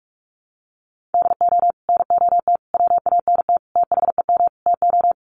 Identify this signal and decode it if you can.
Morse code, heard as a keyed tone, about 44 wpm, 700 Hz. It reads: DO NOT WANT THEM TO
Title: Sets of 5 Words Encore 45wpm